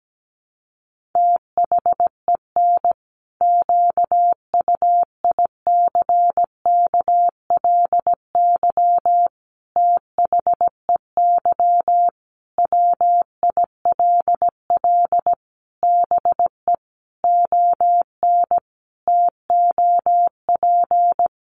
Morse code, heard as a keyed tone, about 17 wpm, 700 Hz. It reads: THEN QUICKLY THEY WILL BE ON TOP
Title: Sentences from Top 700 Words 17wpm